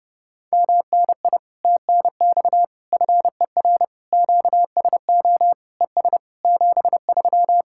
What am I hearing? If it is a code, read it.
Morse code, heard as a keyed tone, about 30 wpm, 700 Hz. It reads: MNI TNX FER QSO ES 73